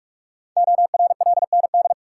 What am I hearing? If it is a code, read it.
Morse code, heard as a keyed tone, about 44 wpm, 700 Hz. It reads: ORLND
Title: Code Groups - 5 Letters - 45wpm